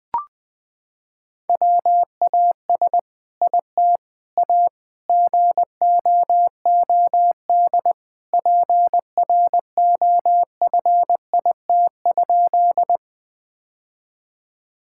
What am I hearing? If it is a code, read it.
Morse code, heard as a keyed tone, about 20 wpm, 700 Hz. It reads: WAS IT A GOOD PROFIT?
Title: Sentences from Top 1000 Words - Review Part 5 - 20wpm